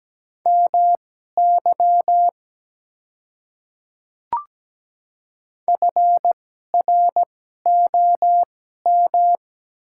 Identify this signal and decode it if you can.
Morse code, heard as a keyed tone, about 17 wpm, spaced as for 12 wpm, 700 Hz. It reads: MY FROM
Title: 2x Character Spacing - Top 100 Words - 17wpm